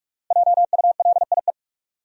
Morse code, heard as a keyed tone, about 45 wpm, 700 Hz. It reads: JULIE
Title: USA Common Names - 45wpm